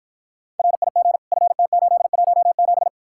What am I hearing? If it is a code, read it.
Morse code, heard as a keyed tone, about 54 wpm, 700 Hz. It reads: RIG FT817